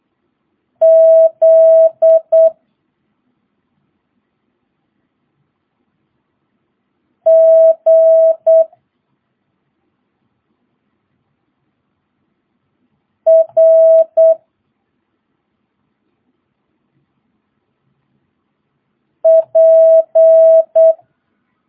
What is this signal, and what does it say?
Morse code, heard as a keyed tone, about 8 wpm, 650 Hz. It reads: Z G R P
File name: jh8cbh / 2022cw